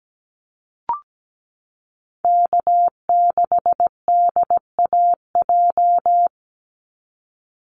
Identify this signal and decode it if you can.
Morse code, heard as a keyed tone, about 17 wpm, 700 Hz. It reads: K6DAJ
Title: Call Signs Repeated 2x - 17wpm